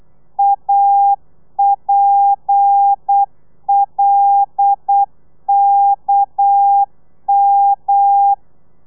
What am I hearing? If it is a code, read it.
Morse code, heard as a keyed tone, about 8 wpm, 800 Hz. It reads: APLKM
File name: telegrafia / i2tfj / Lez11 35